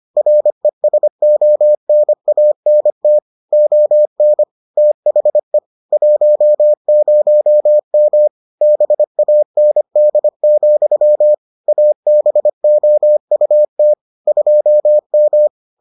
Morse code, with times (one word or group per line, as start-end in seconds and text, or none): 0.2-3.2 RESONANT
3.5-4.4 ON
4.8-5.6 THE
5.9-8.3 10M
8.6-11.3 BAND,
11.7-13.9 ABOUT
14.3-15.5 2M